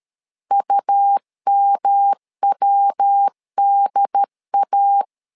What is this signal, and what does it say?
Morse code, heard as a keyed tone, about 13 wpm, 800 Hz. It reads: UMWDA
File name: LEZIONI / 3 MWRO / 3 MW 12wpm 1633